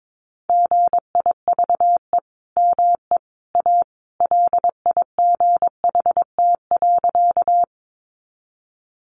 Morse code, heard as a keyed tone, about 22 wpm, 700 Hz. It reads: GIVE ME A LIGHT.
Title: Sentences from Top 200 Words 22wpm